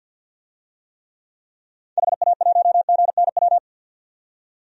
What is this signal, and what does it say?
Morse code, heard as a keyed tone, about 50 wpm, 700 Hz. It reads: HA1GNW
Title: DXCC-100 Call Signs - 6 Characters - 50wpm